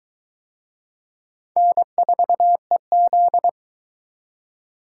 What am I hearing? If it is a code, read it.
Morse code, heard as a keyed tone, about 21 wpm, 700 Hz. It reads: N4EZ